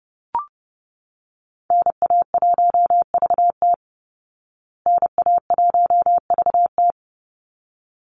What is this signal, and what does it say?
Morse code, heard as a keyed tone, about 30 wpm, 700 Hz. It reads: NA1VT NA1VT
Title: Call Signs Repeated 3x - 30wpm